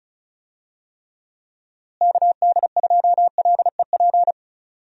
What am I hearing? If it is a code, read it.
Morse code, heard as a keyed tone, about 36 wpm, 700 Hz. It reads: KD2LEP
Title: USA Call Signs - 6 Characters - Repeated 3x - 35wpm